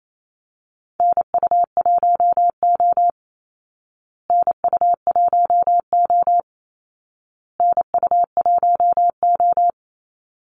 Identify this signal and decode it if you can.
Morse code, heard as a keyed tone, about 28 wpm, 700 Hz. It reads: NU1O NU1O NU1O
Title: Call Signs Repeated 2x - 28wpm